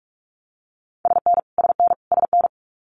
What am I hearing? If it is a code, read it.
Morse code, heard as a keyed tone, about 46 wpm, 700 Hz. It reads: SN SN SN
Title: Multi-letter Phonemes Repeated 3x 45wpm